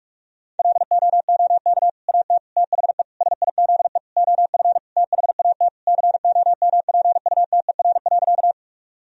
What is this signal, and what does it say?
Morse code, heard as a keyed tone, about 42 wpm, 700 Hz. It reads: LOOK AT THE SIZE OF THAT COMPUTER.
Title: Sentences from Top 1000 Words - Review Part 3 - 45wpm